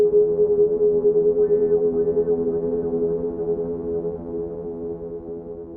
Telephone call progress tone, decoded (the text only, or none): congestion tone